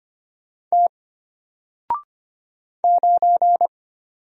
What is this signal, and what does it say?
Morse code, heard as a keyed tone, about 25 wpm, 700 Hz. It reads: T 9
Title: Single Letter-Number Alt 25wpm